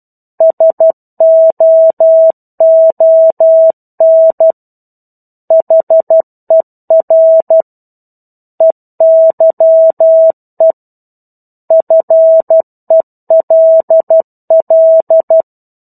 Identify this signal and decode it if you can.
Morse code, heard as a keyed tone, about 12 wpm, 650 Hz. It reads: SOON HER EYE FELL
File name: 01-Chapter-01 12 WPM